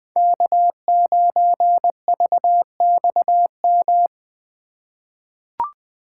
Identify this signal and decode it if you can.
Morse code, heard as a keyed tone, about 20 wpm, 700 Hz. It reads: K9VXM
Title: USA Call Signs - 5 Characters - 3x Character Spacing - 20wpm